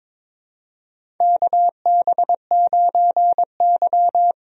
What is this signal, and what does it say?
Morse code, heard as a keyed tone, about 23 wpm, 700 Hz. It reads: KB9Y
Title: USA Call Signs - 4 Characters - 22wpm